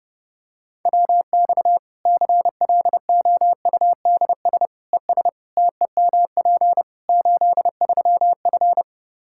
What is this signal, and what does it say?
Morse code, heard as a keyed tone, about 30 wpm, 700 Hz. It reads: WX CLOUDS ES TEMP 83F
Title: K1USN SST QSO Practice 30wpm